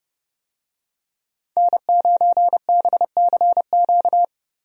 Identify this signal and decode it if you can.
Morse code, heard as a keyed tone, about 30 wpm, 700 Hz. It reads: N9BCQ
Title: USA Call Signs - 5 Characters - 30wpm